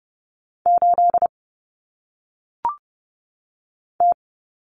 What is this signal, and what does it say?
Morse code, heard as a keyed tone, about 30 wpm, 700 Hz. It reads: 8 T
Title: Common Cut Numbers - 1-Digit 30wpm